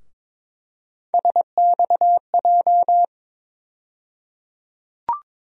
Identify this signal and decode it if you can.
Morse code, heard as a keyed tone, about 22 wpm, 700 Hz. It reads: SXJ
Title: USA Call Sign Postfixes - 3 Letters 22wpm